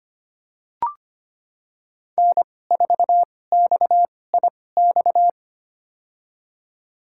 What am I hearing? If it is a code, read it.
Morse code, heard as a keyed tone, about 25 wpm, spaced as for 18 wpm, 700 Hz. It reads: N4XIX
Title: USA Call Signs - 2x Character Spacing - 25wpm